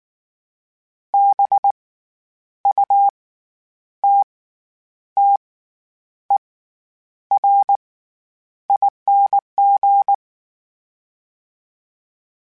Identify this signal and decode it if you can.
Morse code, heard as a keyed tone, about 19 wpm, spaced as for 9 wpm, 800 Hz. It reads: BUTTERING
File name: Special INGs / Special INGs-mp3s / SlowToFaster ing @18